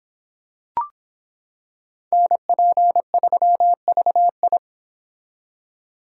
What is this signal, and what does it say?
Morse code, heard as a keyed tone, about 26 wpm, 700 Hz. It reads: NP3VI